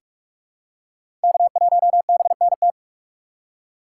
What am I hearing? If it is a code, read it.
Morse code, heard as a keyed tone, about 44 wpm, 690 Hz. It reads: K1BNT
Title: USA Call Signs - 5 Characters - 6x Character Spacing - 45wpm